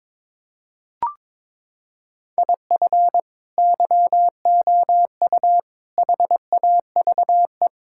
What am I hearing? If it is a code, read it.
Morse code, heard as a keyed tone, about 22 wpm, 700 Hz. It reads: IF YOU HAVE
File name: practice / 22wpm / Sets of 8 Words - 22wpm